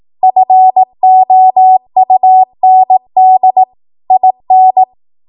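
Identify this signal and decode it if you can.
Morse code, heard as a keyed tone, about 18 wpm, 750 Hz. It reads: FOUND IN
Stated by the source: ARRL 18 wpm Morse code practice archive transmission.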